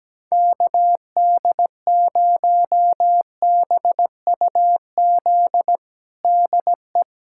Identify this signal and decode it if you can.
Morse code, heard as a keyed tone, about 17 wpm, 700 Hz. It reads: KD0BUZ DE
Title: USA Callsign DE Callsign - 6 Characters - 17wpm